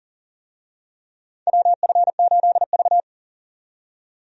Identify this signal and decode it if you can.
Morse code, heard as a keyed tone, about 40 wpm, 700 Hz. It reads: WF8V
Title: USA Call Signs - 4 Characters - 40wpm